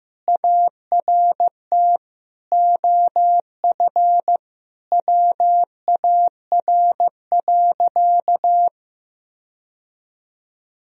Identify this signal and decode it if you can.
Morse code, heard as a keyed tone, about 16 wpm, 700 Hz. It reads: ART OF WAR.
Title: Sentences from Top 1000 Words - Review Part 3 - 15wpm